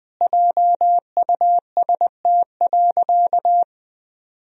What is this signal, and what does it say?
Morse code, heard as a keyed tone, about 21 wpm, 700 Hz. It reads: JUST.
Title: Sentences from Top 1000 Words - Review Part 1 - 20wpm